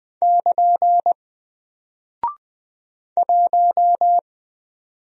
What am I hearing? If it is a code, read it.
Morse code, heard as a keyed tone, about 20 wpm, 700 Hz. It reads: <KN> 1
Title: Single Character (Letter:Number:Punctuation:Prosign) Alt 20wpm